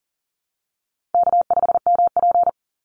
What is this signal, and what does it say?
Morse code, heard as a keyed tone, about 40 wpm, 700 Hz. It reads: K5MP
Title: Easy Call Signs Repeated 2x - 40wpm